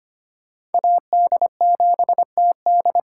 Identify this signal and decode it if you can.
Morse code, heard as a keyed tone, about 32 wpm, 700 Hz. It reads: AD7TD